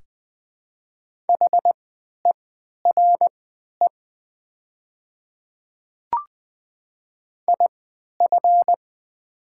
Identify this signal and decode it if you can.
Morse code, heard as a keyed tone, about 20 wpm, spaced as for 11 wpm, 700 Hz. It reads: HERE IF
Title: 3x Character Spacing - Top 200 Words - 20wpm